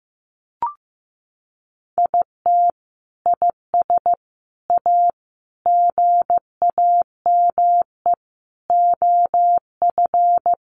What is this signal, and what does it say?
Morse code, heard as a keyed tone, about 15 wpm, 700 Hz. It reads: IT IS A GAME OF
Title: Sentences from Top 1200 Words 15wpm